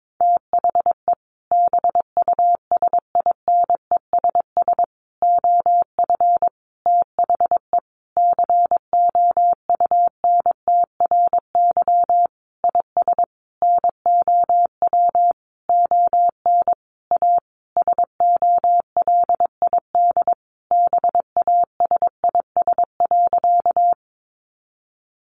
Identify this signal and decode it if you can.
Morse code, heard as a keyed tone, about 22 wpm, 700 Hz. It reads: THE BUSINESS OF THE COUNTRY IS NOW ON A SOLID BASIS.
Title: Sentences from Top 1500 Words 22wpm